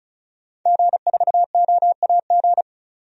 Morse code, heard as a keyed tone, about 36 wpm, 700 Hz. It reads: G4OAG
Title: DXCC-100 Call Signs - 5 Characters - 35wpm